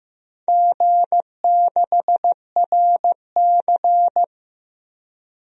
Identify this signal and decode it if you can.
Morse code, heard as a keyed tone, about 15 wpm, 700 Hz. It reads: G6RC
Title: DXCC-100 Call Signs - 4 Characters - Repeated 2x - 15wpm